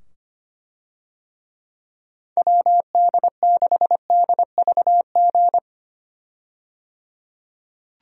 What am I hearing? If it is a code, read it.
Morse code, heard as a keyed tone, about 25 wpm, 700 Hz. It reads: WD6DVG
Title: USA Call Signs - 6 Characters - 25wpm